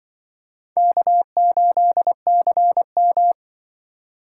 Morse code, heard as a keyed tone, about 24 wpm, 700 Hz. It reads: K8CM
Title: Speed Racing - USA 1x2 Call Signs 20wpm